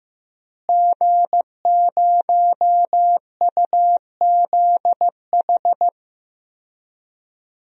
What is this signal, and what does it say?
Morse code, heard as a keyed tone, about 15 wpm, 700 Hz. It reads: G0UZH
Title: DXCC-100 Call Signs - 5 Characters - Repeated 3x - 15wpm